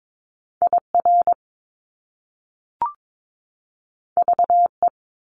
Morse code, heard as a keyed tone, about 22 wpm, 700 Hz. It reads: IR VE
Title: Top Two Letter Combinations in Words 22wpm